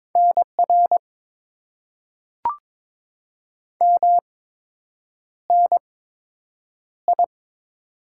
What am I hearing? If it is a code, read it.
Morse code, heard as a keyed tone, about 21 wpm, spaced as for 8 wpm, 700 Hz. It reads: NR MNI
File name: Top 90 QSO Elements - 8x Character Spacing - 22wpm